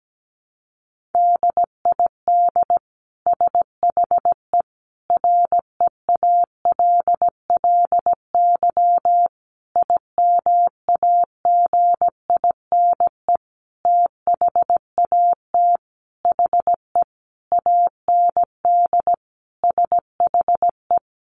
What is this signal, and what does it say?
Morse code, heard as a keyed tone, about 17 wpm, 700 Hz. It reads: DID SHE REALLY IMAGINE THAT HE AND SHE
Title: Sentences from Top 1400 Words 17wpm